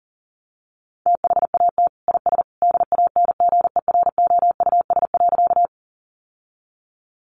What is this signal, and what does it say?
Morse code, heard as a keyed tone, about 40 wpm, 700 Hz. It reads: THAT IS DANGEROUS.
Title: Sentences from Top 1200 Words 40wpm